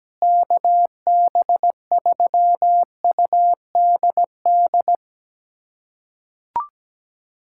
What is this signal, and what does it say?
Morse code, heard as a keyed tone, about 17 wpm, 700 Hz. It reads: KB3UDD
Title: USA Callsign DE Callsign - 6 Characters - 17wpm